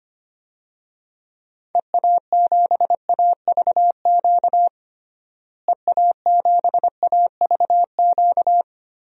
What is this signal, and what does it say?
Morse code, heard as a keyed tone, about 25 wpm, 700 Hz. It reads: EA7AVQ EA7AVQ
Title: DXCC-100 Call Signs - 6 Characters - Repeated 3x - 25wpm